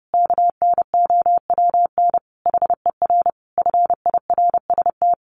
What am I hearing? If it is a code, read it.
Morse code, heard as a keyed tone, about 30 wpm, 700 Hz. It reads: KNOWN HER FIRST
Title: Sentences from Top 1200 Words 30wpm